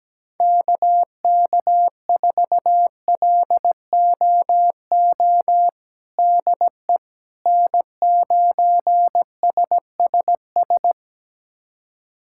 Morse code, heard as a keyed tone, about 17 wpm, 700 Hz. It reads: KK4LOO DE N9SSS
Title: USA Callsign DE Callsign 17wpm